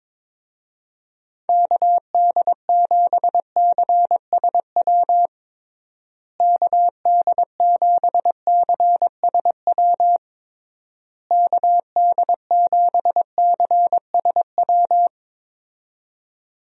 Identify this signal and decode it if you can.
Morse code, heard as a keyed tone, about 22 wpm, 700 Hz. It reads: KD7CSW KD7CSW KD7CSW
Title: USA Call Signs - 6 Characters - Repeated 3x - 22wpm